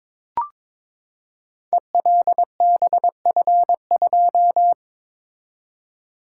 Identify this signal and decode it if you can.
Morse code, heard as a keyed tone, about 22 wpm, 700 Hz. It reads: ELBF2
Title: Code Groups - 5 Letter:Numbers - 22wpm